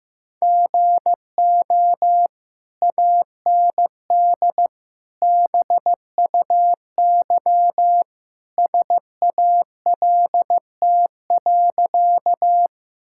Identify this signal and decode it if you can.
Morse code, heard as a keyed tone, about 15 wpm, 700 Hz. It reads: GO AND BUY SALT.
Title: Sentences from Top 1000 Words - Review Part 5 - 15wpm